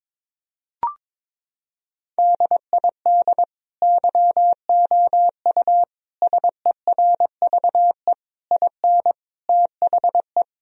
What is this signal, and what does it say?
Morse code, heard as a keyed tone, about 22 wpm, 700 Hz. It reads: DID YOU SERVE IN THE
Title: Sentences from Top 1000 Words - Review Part 4 - 22wpm